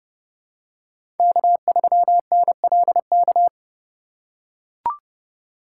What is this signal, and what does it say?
Morse code, heard as a keyed tone, about 30 wpm, 700 Hz. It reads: K3NLK
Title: USA Call Signs - 5 Characters - 4x Character Spacing - 30wpm